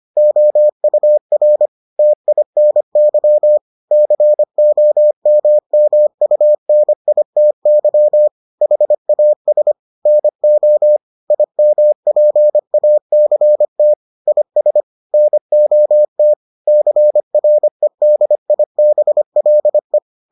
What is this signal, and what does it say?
Morse code, heard as a keyed tone, about 25 wpm, 600 Hz. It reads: OUR TINY COMMUNITY HAS NO IMPACT IS NOT CREDIBLE